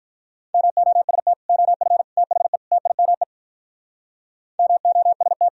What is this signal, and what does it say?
Morse code, heard as a keyed tone, about 45 wpm, 700 Hz. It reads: MOST OF THE TIME MOST